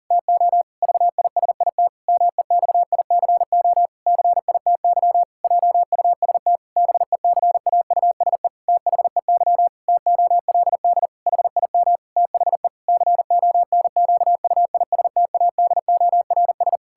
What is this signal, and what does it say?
Morse code, heard as a keyed tone, about 40 wpm, 700 Hz. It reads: TO VISIT MEXICO CITY JUST BECAUSE THEY TOLD HIM THE CONQUISTADORS
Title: Pangram Challenge v2 - 40wpm